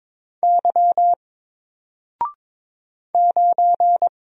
Morse code, heard as a keyed tone, about 22 wpm, 700 Hz. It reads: Y 9
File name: Single-Letter-Number-22wpm